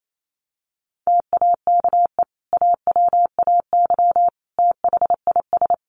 Morse code, heard as a keyed tone, about 28 wpm, 700 Hz. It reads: TAKE AWAY THIS
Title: Sentences from Top 1300 Words 28wpm